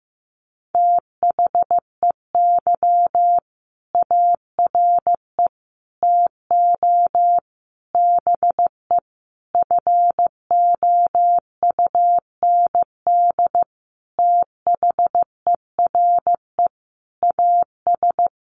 Morse code, heard as a keyed tone, about 15 wpm, 700 Hz. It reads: THEY ARE TO BE FOUND THERE AS
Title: Sentences from Top 1400 Words 15wpm